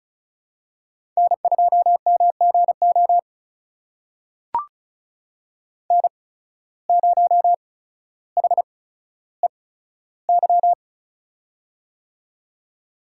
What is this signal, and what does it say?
Morse code, heard as a keyed tone, about 36 wpm, spaced as for 15 wpm, 700 Hz. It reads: N2MGO N0HEY
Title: USA Call Signs - 5 Characters - 8x Character Spacing - 35wpm